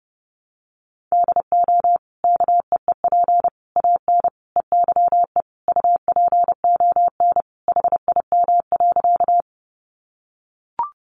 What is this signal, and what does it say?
Morse code, heard as a keyed tone, about 30 wpm, 700 Hz. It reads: DO KEEP AN EYE UPON HIM.
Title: Sentences from Top 1100 Words 30wpm